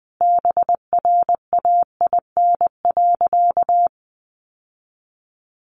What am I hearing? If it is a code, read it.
Morse code, heard as a keyed tone, about 19 wpm, 700 Hz. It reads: BRAIN.
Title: Sentences from Top 1100 Words 20wpm